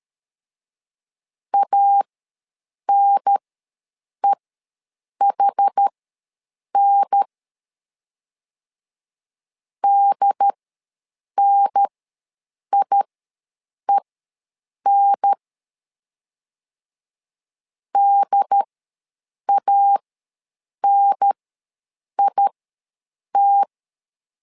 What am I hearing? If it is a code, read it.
Morse code, heard as a keyed tone, about 13 wpm, spaced as for 7 wpm, 800 Hz. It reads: ANEHN DNIEN DANIT
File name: LEZIONI / 2 UHDN / 2 UHDN ws 12wpm 1708